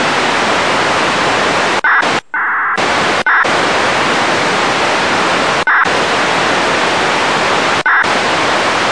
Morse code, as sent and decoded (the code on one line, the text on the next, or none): none